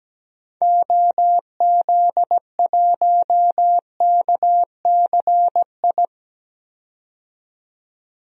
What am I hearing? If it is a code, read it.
Morse code, heard as a keyed tone, about 17 wpm, 700 Hz. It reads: OZ1KCI
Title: DXCC-100 Call Signs - 6 Characters - 17wpm